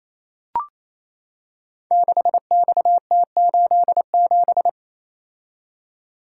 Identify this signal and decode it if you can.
Morse code, heard as a keyed tone, about 28 wpm, 700 Hz. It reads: 6XT87
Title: Code Groups - 5 Letter:Numbers - 28wpm